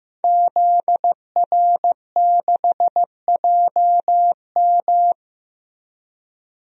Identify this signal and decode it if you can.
Morse code, heard as a keyed tone, about 15 wpm, 700 Hz. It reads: ZR6JM
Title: DXCC-100 Call Signs - 5 Characters - Repeated 2x - 15wpm